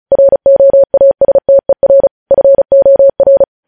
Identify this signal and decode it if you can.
Morse code, heard as a keyed tone, about 36 wpm, 550 Hz. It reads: ROASTER FOR
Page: News headlines in Morse code at 35 wpm